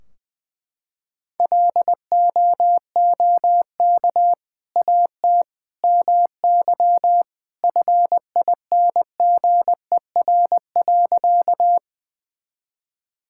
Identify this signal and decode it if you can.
Morse code, heard as a keyed tone, about 20 wpm, 700 Hz. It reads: LOOK AT MY FINGER.
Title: Sentences from Top 1600 Words 20wpm